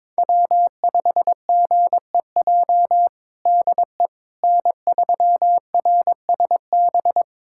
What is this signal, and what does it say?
Morse code, heard as a keyed tone, about 22 wpm, 700 Hz. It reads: W5GEJ DE N3RSB
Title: USA Callsign DE Callsign - 5 Characters - 22wpm